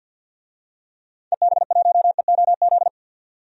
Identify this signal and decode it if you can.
Morse code, heard as a keyed tone, about 51 wpm, 700 Hz. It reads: EB1EOZ
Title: DXCC-100 Call Signs - 6 Characters - 50wpm